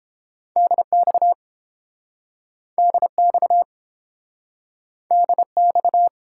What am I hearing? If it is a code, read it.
Morse code, heard as a keyed tone, about 30 wpm, 700 Hz. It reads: DX DX DX